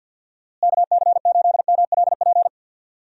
Morse code, heard as a keyed tone, about 51 wpm, 700 Hz. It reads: KC8MLP
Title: USA Call Signs - 6 Characters - Repeated 3x - 50wpm